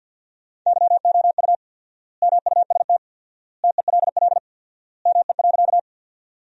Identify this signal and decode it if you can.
Morse code, heard as a keyed tone, about 50 wpm, 700 Hz. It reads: YOU MUST TELL ME.
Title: 4x Word-Spacing - Sentences from Top 200 Words - 50wpm